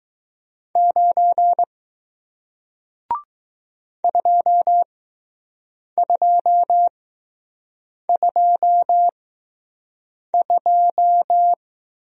Morse code, heard as a keyed tone, about 20 wpm, spaced as for 6 wpm, 700 Hz. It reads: 9 2222